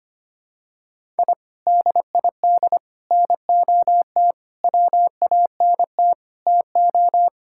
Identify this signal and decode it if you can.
Morse code, heard as a keyed tone, about 25 wpm, 700 Hz. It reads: I DID NOT WANT TO
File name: practice / 25wpm / Sentences with 4-Word Phrases - 25wpm